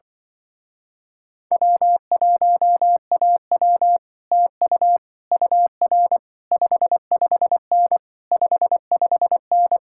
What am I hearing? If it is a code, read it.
Morse code, heard as a keyed tone, about 24 wpm, 700 Hz. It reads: W1AW TU UR 55N 55N
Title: Speed-Racing Common QSO Phrases Encore - 20wpm